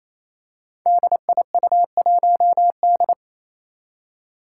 Morse code, heard as a keyed tone, about 28 wpm, 700 Hz. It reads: DIU1D
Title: Five Challenging Letter-Number Combinations 28wpm